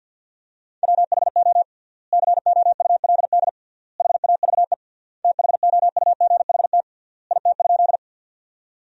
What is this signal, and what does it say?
Morse code, heard as a keyed tone, about 53 wpm, 700 Hz. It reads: WHO COULD HAVE THOUGHT IT?
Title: 3x Word-Spacing - Sentences from Top 300 Words - 50wpm